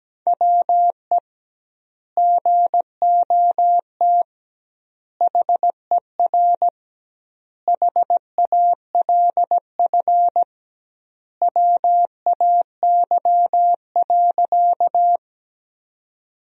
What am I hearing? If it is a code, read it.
Morse code, heard as a keyed tone, about 17 wpm, 700 Hz. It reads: WE GOT HER HALF WAY.
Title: 2x Word-Spacing - Sentences from Top 400 Words - 17wpm